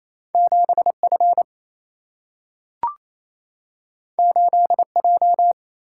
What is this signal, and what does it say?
Morse code, heard as a keyed tone, about 28 wpm, 700 Hz. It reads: 7F 8J
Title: Single-Letter Single-Number 28wpm